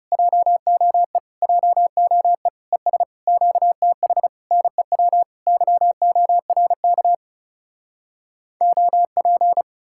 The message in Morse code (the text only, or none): JOE JOE ES QTH NEW YORK OP